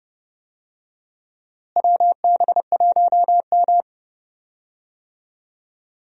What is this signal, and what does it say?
Morse code, heard as a keyed tone, about 30 wpm, 700 Hz. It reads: WB1M
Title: USA Call Signs - 4 Characters - Repeated 2x - 30wpm